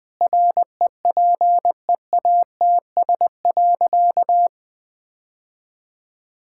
Morse code, heard as a keyed tone, about 20 wpm, 700 Hz. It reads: REPEATS.
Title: Sentences with 6-Word Phrases - Spoken Beginning - 20wpm